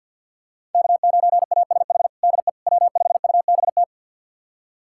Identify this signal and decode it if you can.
Morse code, heard as a keyed tone, about 52 wpm, 700 Hz. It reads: K9ASH DE W5UBT
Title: USA Callsign DE Callsign - 5 Characters - 50wpm